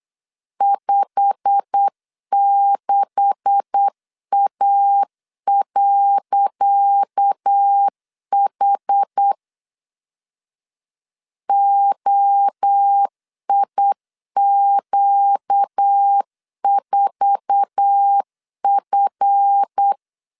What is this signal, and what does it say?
Morse code, heard as a keyed tone, about 8 wpm, 800 Hz. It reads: 56A.H OIQ4F
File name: LEZIONI / 10 SIGNS / 10C DOT 8wpm 3649